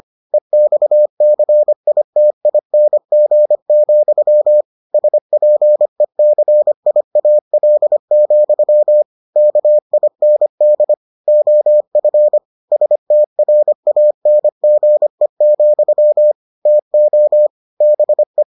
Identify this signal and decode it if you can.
Morse code, heard as a keyed tone, about 25 wpm, 600 Hz. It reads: EXCITING, SPECIAL, KIND OF STRANGE, TO BE